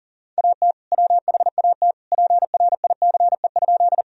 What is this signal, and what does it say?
Morse code, heard as a keyed tone, about 40 wpm, 700 Hz. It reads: AT WHAT PRICE?
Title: Sentences from Top 1000 Words - Review Part 4 - 40wpm